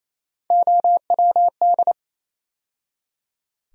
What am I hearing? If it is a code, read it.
Morse code, heard as a keyed tone, about 28 wpm, 700 Hz. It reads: OWD
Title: USA Call Sign Postfixes - 3 Letters 28wpm